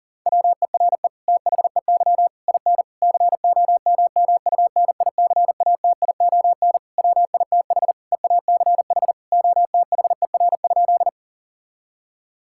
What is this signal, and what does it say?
Morse code, heard as a keyed tone, about 40 wpm, 700 Hz. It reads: WERE THEY IN COMMUNICATION WITH EACH OTHER?
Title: Sentences from Top 1000 Words - Review Part 6 - 40wpm